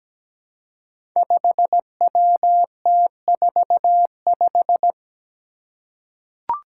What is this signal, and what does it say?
Morse code, heard as a keyed tone, about 17 wpm, 700 Hz. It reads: 5WT45
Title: Five Challenging Letter-Number Combinations 17wpm